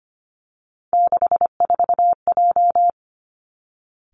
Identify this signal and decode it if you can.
Morse code, heard as a keyed tone, about 25 wpm, 700 Hz. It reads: 64J
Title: Three Challenging Letter-Number Combinations 25wpm